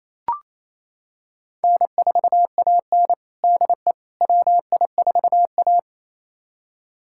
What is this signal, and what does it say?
Morse code, heard as a keyed tone, about 28 wpm, 700 Hz. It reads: N4AN DE WI4A
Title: USA Callsign DE Callsign - 4 Characters - 28wpm